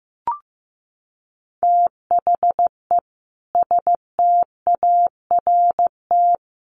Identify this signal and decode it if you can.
Morse code, heard as a keyed tone, about 15 wpm, 700 Hz. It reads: THE START
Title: Sets of 4 Words 15wpm